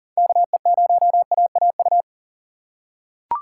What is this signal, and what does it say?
Morse code, heard as a keyed tone, about 37 wpm, 700 Hz. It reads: KE0AAU
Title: USA Call Signs - 4x Character Spacing - 40wpm